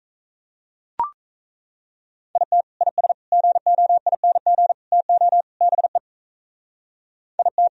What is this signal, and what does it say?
Morse code, heard as a keyed tone, about 34 wpm, 700 Hz. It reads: IT IS GOING TO BE IT
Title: Speed-Racing - Sets of 5 Words 28wpm